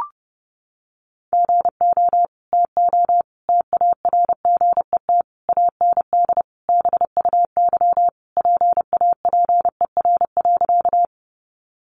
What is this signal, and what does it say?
Morse code, heard as a keyed tone, about 30 wpm, 700 Hz. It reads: GO TO TARGET AND BUY PAPER.
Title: Sentences from Top 1200 Words 30wpm